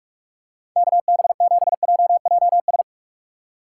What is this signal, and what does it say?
Morse code, heard as a keyed tone, about 46 wpm, 700 Hz. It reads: KB7JJS